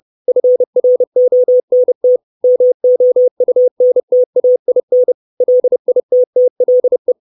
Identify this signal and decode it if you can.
Morse code, heard as a keyed tone, about 30 wpm, 500 Hz. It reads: FRONT MOUNTAIN LITTLE
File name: 632 Words 30WPM